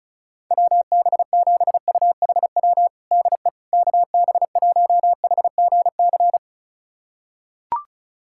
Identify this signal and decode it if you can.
Morse code, heard as a keyed tone, about 34 wpm, 700 Hz. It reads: WB7UHW DE KB1HGC
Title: USA Callsign DE Callsign 35wpm